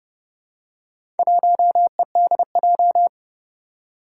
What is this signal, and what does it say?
Morse code, heard as a keyed tone, about 30 wpm, 700 Hz. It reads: 1EDJ
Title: Four Challenging Letter-Number Combinations 30wpm